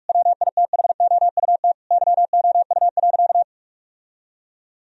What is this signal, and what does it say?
Morse code, heard as a keyed tone, about 46 wpm, 700 Hz. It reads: WITHOUT YOU.
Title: Sentences from Top 1000 Words - Review Part 3 - 45wpm